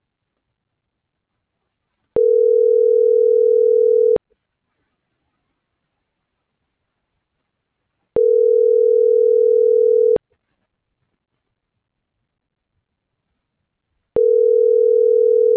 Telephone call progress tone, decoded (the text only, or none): ringback tone